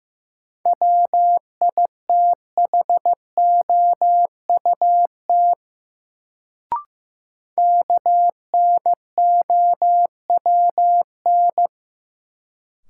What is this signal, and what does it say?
Morse code, heard as a keyed tone, about 16 wpm, 700 Hz. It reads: WITHOUT KNOWN
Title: Sentences from Top 600 Words - New Word Drill - 15wpm